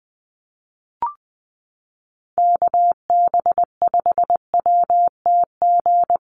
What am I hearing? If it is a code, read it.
Morse code, heard as a keyed tone, about 20 wpm, 700 Hz. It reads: KB5WTG
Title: USA Call Signs Repeated 3x 20wpm